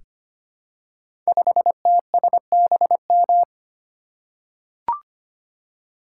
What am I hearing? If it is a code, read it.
Morse code, heard as a keyed tone, about 25 wpm, 700 Hz. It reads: 5TSBM
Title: Code Groups - 5 Letter:Numbers - 25wpm